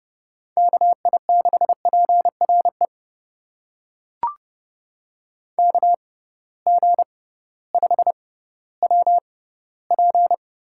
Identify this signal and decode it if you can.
Morse code, heard as a keyed tone, about 30 wpm, spaced as for 16 wpm, 700 Hz. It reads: KI6PRE KG5WP
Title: USA Call Signs - 6 Characters - 6x Character Spacing - 30wpm